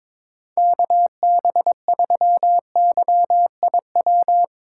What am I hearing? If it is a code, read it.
Morse code, heard as a keyed tone, about 24 wpm, 700 Hz. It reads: KB3YIW
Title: Speed Racing - USA 2x3 Call Signs 22wpm